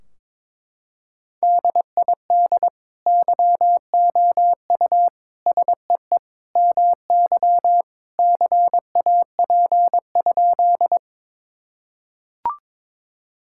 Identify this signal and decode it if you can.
Morse code, heard as a keyed tone, about 22 wpm, 700 Hz. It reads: DID YOU SEE MY CAP?
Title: Sentences from Top 1600 Words 22wpm